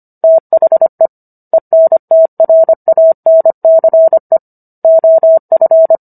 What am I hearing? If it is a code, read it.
Morse code, heard as a keyed tone, about 25 wpm, 650 Hz. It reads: THE ENTRANCE OF
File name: Book One / 16-Chapter 16 - 25 WPM